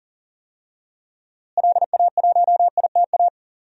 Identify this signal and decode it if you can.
Morse code, heard as a keyed tone, about 40 wpm, 700 Hz. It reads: LA1ITA
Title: DXCC-100 Call Signs - 6 Characters - Repeated 3x - 40wpm